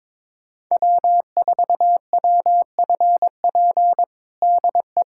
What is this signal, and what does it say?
Morse code, heard as a keyed tone, about 22 wpm, 700 Hz. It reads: W4WFP DE
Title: USA Callsign DE Callsign - 5 Characters - 22wpm